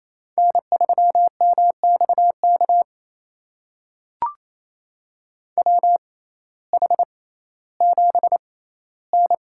Morse code, heard as a keyed tone, about 28 wpm, spaced as for 15 wpm, 700 Hz. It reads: N3MXK WH7N
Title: USA Call Signs - 5 Characters - 6x Character Spacing - 28wpm